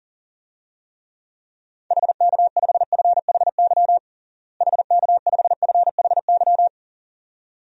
Morse code, heard as a keyed tone, about 40 wpm, 700 Hz. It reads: HK5FHY HK5FHY
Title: DXCC-100 Call Signs - 6 Characters - Repeated 2x - 40wpm